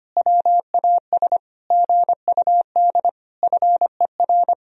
Morse code, heard as a keyed tone, about 25 wpm, 700 Hz. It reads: WAS GUD FER
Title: Sentences from Top 400 Words and Abbreviations 25wpm